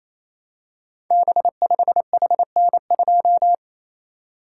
Morse code, heard as a keyed tone, about 27 wpm, 700 Hz. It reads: B5HN2
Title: Code Groups - 5 Letter:Numbers - 28wpm